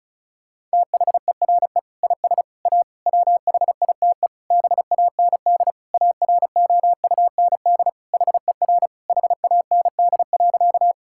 Morse code, heard as a keyed tone, about 34 wpm, 700 Hz. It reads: THERE IS A WHITE BAND AROUND HER HAND.
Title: Sentences from Top 1600 Words 35wpm